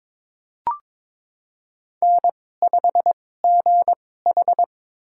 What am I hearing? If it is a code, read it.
Morse code, heard as a keyed tone, about 23 wpm, spaced as for 16 wpm, 700 Hz. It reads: N5GH
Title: USA Call Signs - 4 Characters - 2x Character Spacing - 22wpm